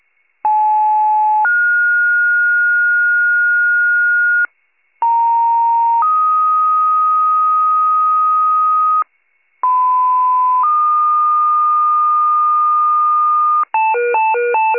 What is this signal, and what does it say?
Two-tone page, heard as a one-second tone, then a three-second tone, about 860 then 1430 Hz, about 930 then 1250 Hz, about 1010 then 1220 Hz. It is heard from 0.4 to 13.6 s.
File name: 82 FIRE